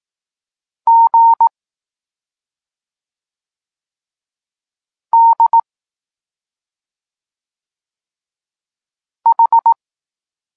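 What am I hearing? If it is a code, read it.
Morse code, heard as a keyed tone, about 18 wpm, 920 Hz. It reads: G D H
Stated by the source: Morse code lesson for learning the letters K and X, from the SP9MOA Niepołomicki Klub Krótkofalowców course.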